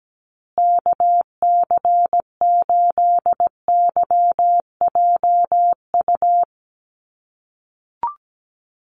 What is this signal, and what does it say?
Morse code, heard as a keyed tone, about 17 wpm, 700 Hz. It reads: KC8YJU
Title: USA Call Signs Repeated 3x 17wpm